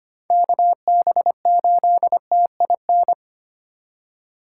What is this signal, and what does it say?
Morse code, heard as a keyed tone, about 25 wpm, 700 Hz. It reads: KB8TIN
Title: USA Call Signs - 6 Characters - 3x Character Spacing - 25wpm